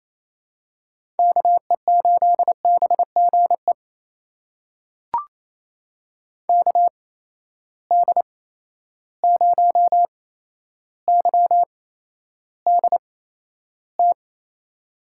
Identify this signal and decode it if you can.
Morse code, heard as a keyed tone, about 28 wpm, spaced as for 12 wpm, 700 Hz. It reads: KE8BGE KD0YDT
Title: USA Call Signs - 6 Characters - 8x Character Spacing - 28wpm